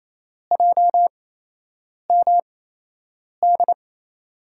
Morse code, heard as a keyed tone, about 29 wpm, spaced as for 8 wpm, 700 Hz. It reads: JMD
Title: USA Call Signs - 5 Characters - 8x Character Spacing - 28wpm